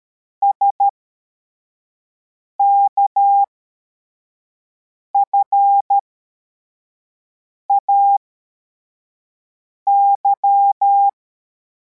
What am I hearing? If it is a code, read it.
Morse code, heard as a keyed tone, about 13 wpm, spaced as for 4 wpm, 800 Hz. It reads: SKFAY
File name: formation / cw / Text63